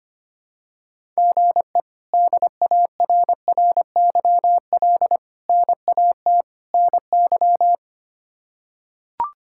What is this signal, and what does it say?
Morse code, heard as a keyed tone, about 25 wpm, 700 Hz. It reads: GE DARRYL NAT NY
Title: K1USN SST QSO Practice - 25wpm